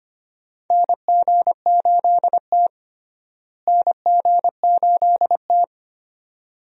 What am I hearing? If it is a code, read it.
Morse code, heard as a keyed tone, about 25 wpm, 700 Hz. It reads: NG8T NG8T
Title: USA Call Signs - 4 Characters - Repeated 2x - 25wpm